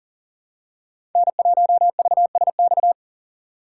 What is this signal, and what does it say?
Morse code, heard as a keyed tone, about 40 wpm, 700 Hz. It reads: N1VSX
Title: USA Call Signs - 3x Character Spacing - 40wpm